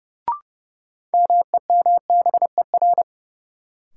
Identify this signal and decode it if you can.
Morse code, heard as a keyed tone, about 30 wpm, 700 Hz. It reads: MEMBER
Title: Sets of 3 Words - New Word Drill - 30wpm